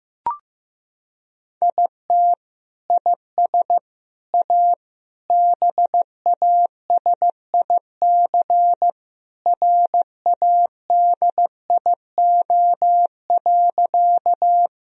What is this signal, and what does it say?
Morse code, heard as a keyed tone, about 15 wpm, 700 Hz. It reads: IT IS A BASIC RADIO.
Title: Sentences from Top 1000 Words - Review Part 3 - 15wpm